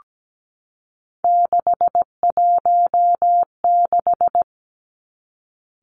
Morse code, heard as a keyed tone, about 17 wpm, 700 Hz. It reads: 616
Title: Three Challenging Letter-Number Combinations 17wpm